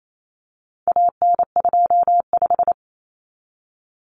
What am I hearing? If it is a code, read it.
Morse code, heard as a keyed tone, about 28 wpm, 700 Hz. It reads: AN25
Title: Common Cut Numbers - 4-Digit 28wpm